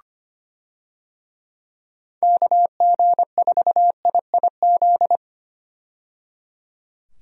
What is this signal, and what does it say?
Morse code, heard as a keyed tone, about 25 wpm, 700 Hz. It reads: KG4IIZ
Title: USA Call Signs - 6 Characters - 25wpm